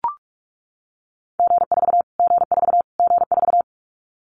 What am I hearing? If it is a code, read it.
Morse code, heard as a keyed tone, about 44 wpm, 700 Hz. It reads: G4 G4 G4
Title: Callsign Prefixes for Top 25 DXCC - Repeated 3x - 45wpm